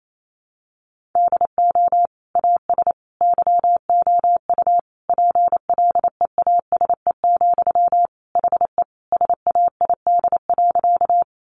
Morse code, heard as a keyed tone, about 28 wpm, 700 Hz. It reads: DO AS YOU PLEASE, HE SAID.
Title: Sentences from Top 1500 Words 28wpm